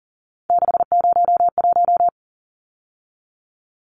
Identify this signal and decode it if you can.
Morse code, heard as a keyed tone, about 40 wpm, 700 Hz. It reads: 601
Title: Numbers - 3-Digit 40wpm